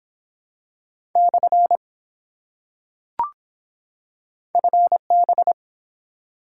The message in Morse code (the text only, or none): / FB